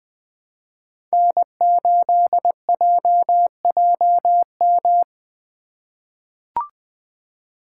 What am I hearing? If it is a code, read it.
Morse code, heard as a keyed tone, about 20 wpm, 700 Hz. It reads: N8JJM
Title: 2023 POTA Hunter Call Signs - 20wpm